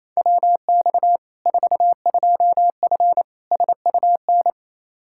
Morse code, heard as a keyed tone, about 28 wpm, 700 Hz. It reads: WX 42F SUN
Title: K1USN SST QSO Practice 28wpm